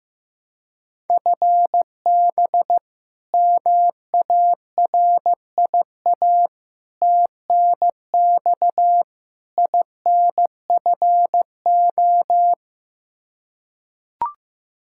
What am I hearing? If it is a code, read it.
Morse code, heard as a keyed tone, about 15 wpm, 700 Hz. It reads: FB MARIA TNX INFO
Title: Practice QSO - Exchange 3 - More Info - 15wpm